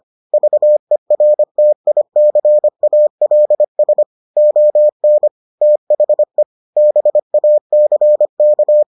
Morse code, heard as a keyed tone, about 25 wpm, 600 Hz. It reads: VERTICALS ON THE BACK